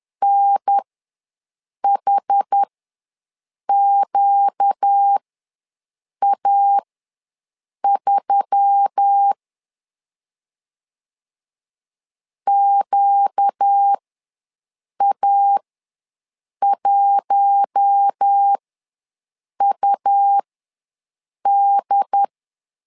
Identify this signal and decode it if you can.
Morse code, heard as a keyed tone, about 10 wpm, spaced as for 6 wpm, 800 Hz. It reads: NHQA3 QA1UD
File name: LEZIONI / 4 Q1S3 ws 10wpm 2437